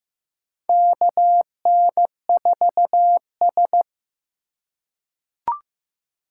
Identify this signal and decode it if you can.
Morse code, heard as a keyed tone, about 15 wpm, 700 Hz. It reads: KN4S
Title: USA Call Signs - 4 Characters - 3x Character Spacing - 15wpm